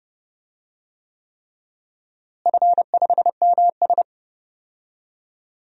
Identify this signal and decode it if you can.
Morse code, heard as a keyed tone, about 30 wpm, 700 Hz. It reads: F5MS
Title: DXCC-100 Call Signs - 4 Characters - Repeated 3x - 30wpm